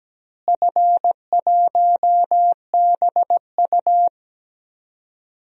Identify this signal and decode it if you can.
Morse code, heard as a keyed tone, about 17 wpm, 700 Hz. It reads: F1BU
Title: DXCC-100 Call Signs - 4 Characters - Repeated 3x - 17wpm